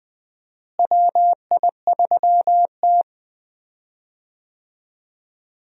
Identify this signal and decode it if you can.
Morse code, heard as a keyed tone, about 20 wpm, 700 Hz. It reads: WI3T
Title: USA Call Signs - 4 Characters - Repeated 3x - 20wpm